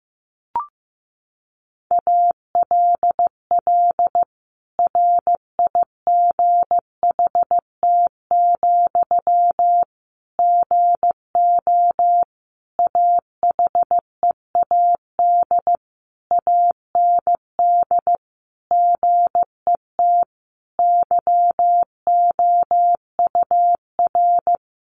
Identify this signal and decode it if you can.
Morse code, heard as a keyed tone, about 15 wpm, 700 Hz. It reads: ALL RIGHT, GO AHEAD AND GET YOUR
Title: Sentences from Top 1100 Words 15wpm